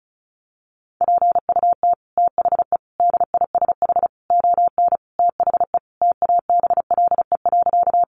Morse code, heard as a keyed tone, about 34 wpm, 700 Hz. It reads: PUT THE DISH ON THE TABLE.
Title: Sentences from Top 1300 Words 35wpm